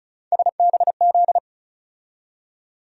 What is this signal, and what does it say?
Morse code, heard as a keyed tone, about 36 wpm, 700 Hz. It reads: SBZ
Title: USA Call Sign Postfixes - 3 Letters 35wpm